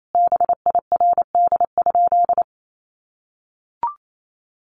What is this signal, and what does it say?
Morse code, heard as a keyed tone, about 28 wpm, 700 Hz. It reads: BIRD?
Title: Sentences from Top 400 Words 28wpm